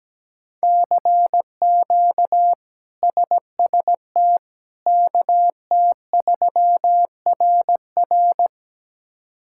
Morse code, heard as a keyed tone, about 17 wpm, 700 Hz. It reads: CQ SST KT3RR
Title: K1USN SST QSO Practice - 17wpm